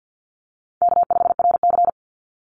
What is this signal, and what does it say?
Morse code, heard as a keyed tone, about 50 wpm, 700 Hz. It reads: K5RC
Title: Call Signs - 50wpm